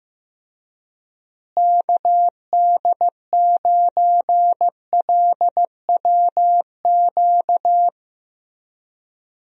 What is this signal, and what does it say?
Morse code, heard as a keyed tone, about 15 wpm, 700 Hz. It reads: KD9LWQ
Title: USA Call Signs - 6 Characters - Repeated 3x - 15wpm